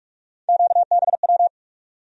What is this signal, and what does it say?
Morse code, heard as a keyed tone, about 47 wpm, 700 Hz. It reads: QBW
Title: USA Call Sign Postfixes - 3 Letters 45wpm